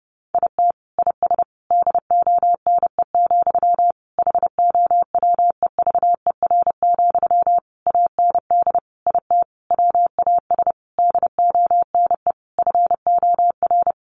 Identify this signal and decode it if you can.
Morse code, heard as a keyed tone, about 30 wpm, 700 Hz. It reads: IT IS DONE, HOWEVER, AND IT WAS DONE FOR
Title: Sentences from Top 600 Words 30wpm